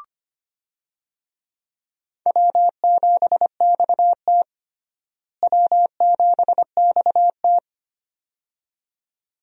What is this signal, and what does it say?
Morse code, heard as a keyed tone, about 25 wpm, 700 Hz. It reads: W7XT W7XT
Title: USA Call Signs - 4 Characters - Repeated 2x - 25wpm